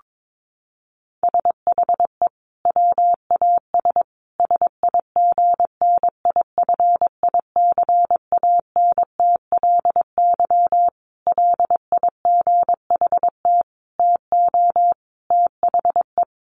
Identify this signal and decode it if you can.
Morse code, heard as a keyed tone, about 22 wpm, 700 Hz. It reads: SHE WAS SIGNIFICANTLY LIGHT TO THE
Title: Sentences from Top 1500 Words 22wpm